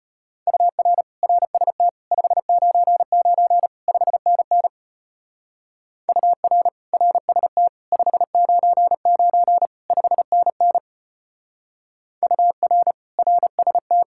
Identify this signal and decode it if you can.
Morse code, heard as a keyed tone, about 32 wpm, 700 Hz. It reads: UR RST 599 5NN UR RST 599 5NN UR RST